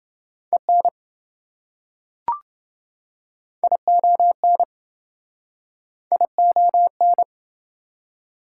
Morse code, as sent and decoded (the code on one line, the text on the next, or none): . -. / .. --- -. / .. --- -.
EN ION ION